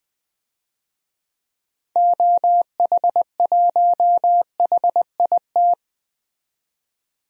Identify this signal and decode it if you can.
Morse code, heard as a keyed tone, about 20 wpm, 700 Hz. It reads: OH1HIT
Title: DXCC-100 Call Signs - 6 Characters - 20wpm